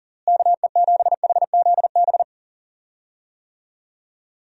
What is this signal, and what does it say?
Morse code, heard as a keyed tone, about 40 wpm, 700 Hz. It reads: KE7HZB
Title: USA Call Signs - 6 Characters - Repeated 2x - 40wpm